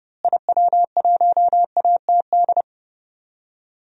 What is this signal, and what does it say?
Morse code, heard as a keyed tone, about 30 wpm, 700 Hz. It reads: IW1ATD
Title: DXCC-100 Call Signs - 6 Characters - Repeated 3x - 30wpm